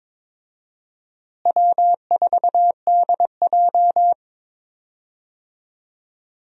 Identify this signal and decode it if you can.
Morse code, heard as a keyed tone, about 22 wpm, 700 Hz. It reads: W4DJ
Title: USA Call Signs - 4 Characters - Repeated 3x - 22wpm